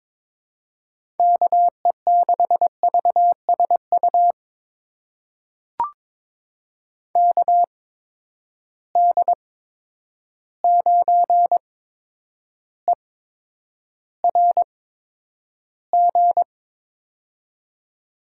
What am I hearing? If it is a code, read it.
Morse code, heard as a keyed tone, about 21 wpm, spaced as for 9 wpm, 700 Hz. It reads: KE6VSU KD9ERG
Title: USA Call Signs - 6 Characters - 8x Character Spacing - 22wpm